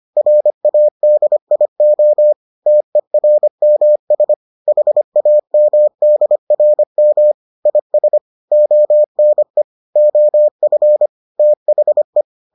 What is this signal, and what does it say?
Morse code, heard as a keyed tone, about 25 wpm, 600 Hz. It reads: RADIO TERMS HAMDRM IS ONE OF THE